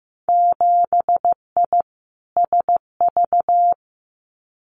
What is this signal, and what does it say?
Morse code, heard as a keyed tone, about 15 wpm, 700 Hz. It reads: 7I SV
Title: Winter Field Day Exchanges Encore 15wpm